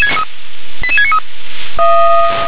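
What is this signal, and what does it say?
Five-tone selective call (ZVEI), heard as quick runs of five tones, tones of about 70 ms. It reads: 88693
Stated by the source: ZVEI five-tone selective call 88788 received on an SDR.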